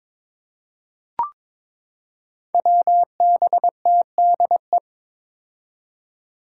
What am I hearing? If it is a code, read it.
Morse code, heard as a keyed tone, about 22 wpm, 700 Hz. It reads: WBTDE
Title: Five Challenging Letter-Number Combinations 22wpm